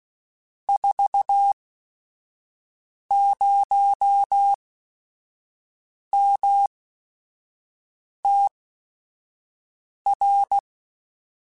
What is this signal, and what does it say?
Morse code, heard as a keyed tone, about 16 wpm, spaced as for 5 wpm, 800 Hz. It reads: 40MTR